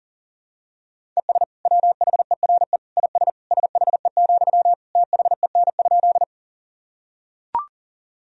Morse code, heard as a keyed tone, about 40 wpm, 700 Hz. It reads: ES WHERE IS SHE, THEN?
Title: Sentences from Top 200 Words and Abbreviations 40wpm